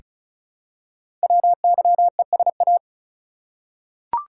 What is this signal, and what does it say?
Morse code, heard as a keyed tone, about 36 wpm, 700 Hz. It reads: WYESA
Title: Code Groups - 5 Letters - 35wpm